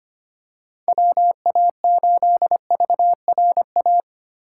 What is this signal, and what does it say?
Morse code, heard as a keyed tone, about 25 wpm, 700 Hz. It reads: WA8VRA
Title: USA Call Signs - 4x Character Spacing - 25wpm